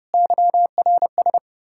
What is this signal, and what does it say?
Morse code, heard as a keyed tone, about 38 wpm, 700 Hz. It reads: YRS